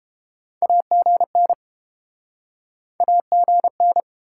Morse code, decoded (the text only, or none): AGN AGN